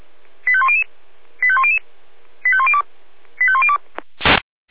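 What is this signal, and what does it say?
Five-tone selective call (ZVEI), heard as quick runs of five tones, tones of about 70 ms. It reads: 85100 85100 85102 85102
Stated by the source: ZVEI five-tone selective call received on an SDR.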